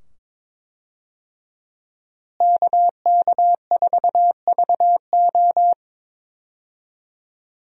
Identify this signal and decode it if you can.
Morse code, heard as a keyed tone, about 22 wpm, 700 Hz. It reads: KK4VO
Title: USA Call Signs - 5 Characters - Repeated 2x - 22wpm